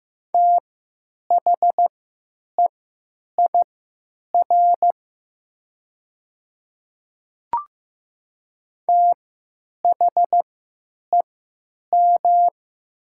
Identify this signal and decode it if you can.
Morse code, heard as a keyed tone, about 15 wpm, spaced as for 8 wpm, 700 Hz. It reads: THEIR THEM
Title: 3x Character Spacing - Top 200 Words - 15wpm